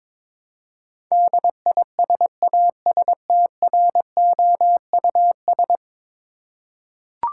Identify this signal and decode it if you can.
Morse code, heard as a keyed tone, about 21 wpm, 700 Hz. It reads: DISASTROUS
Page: morse-code-ninja-practice-22wpm